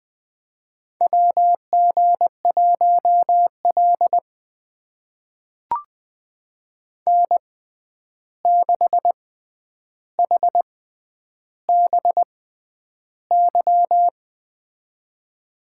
Morse code, heard as a keyed tone, about 21 wpm, spaced as for 10 wpm, 700 Hz. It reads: WG1L N6HBY
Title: USA Call Signs - 6x Character Spacing - 20wpm